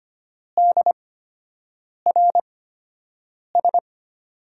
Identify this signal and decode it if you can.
Morse code, heard as a keyed tone, about 25 wpm, spaced as for 7 wpm, 700 Hz. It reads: DRS
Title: USA Call Signs - 5 Characters - 8x Character Spacing - 25wpm